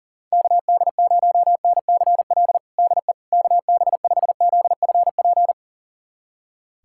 Morse code, heard as a keyed tone, about 40 wpm, 700 Hz. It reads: KD0NCL DE KB5ZFP
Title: USA Callsign DE Callsign - 6 Characters - 40wpm